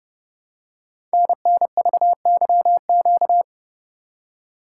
Morse code, heard as a keyed tone, about 30 wpm, 700 Hz. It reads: NNVYQ
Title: Code Groups - 5 Letters - 30wpm